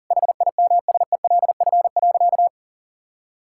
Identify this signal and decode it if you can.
Morse code, heard as a keyed tone, about 40 wpm, 700 Hz. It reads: HIMSELF.
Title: Sentences from Top 1600 Words 40wpm